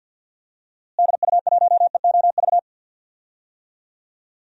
Morse code, heard as a keyed tone, about 50 wpm, 700 Hz. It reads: DU1EOV
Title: DXCC-100 Call Signs - 6 Characters - Repeated 2x - 50wpm